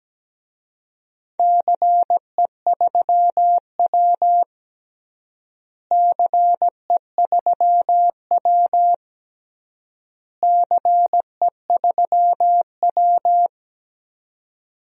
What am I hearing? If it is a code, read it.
Morse code, heard as a keyed tone, about 17 wpm, 700 Hz. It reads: CE3W CE3W CE3W
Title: DXCC-100 Call Signs - 4 Characters - Repeated 3x - 17wpm